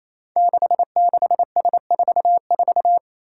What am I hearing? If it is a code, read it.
Morse code, heard as a keyed tone, about 27 wpm, 700 Hz. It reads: 66S44
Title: Five Challenging Letter-Number Combinations 28wpm